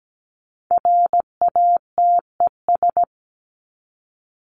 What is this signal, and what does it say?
Morse code, heard as a keyed tone, about 17 wpm, 700 Hz. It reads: RATES
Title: ES:S Suffix - Top 100 Words with Base Words 17wpm